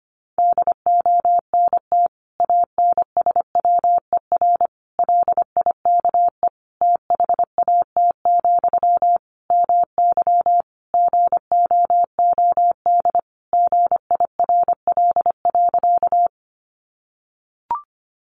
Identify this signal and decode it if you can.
Morse code, heard as a keyed tone, about 25 wpm, 700 Hz. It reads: DONT ANSWER LIKE THAT, MY GOOD GIRL.
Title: Sentences from Top 400 Words 25wpm